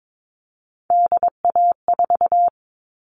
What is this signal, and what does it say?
Morse code, heard as a keyed tone, about 22 wpm, 700 Hz. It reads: DA4
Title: All Cut Numbers - 3-Digit 22wpm